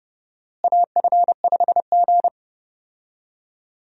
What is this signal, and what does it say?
Morse code, heard as a keyed tone, about 30 wpm, 700 Hz. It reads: AF5G
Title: USA Call Signs - 4 Characters - Repeated 2x - 30wpm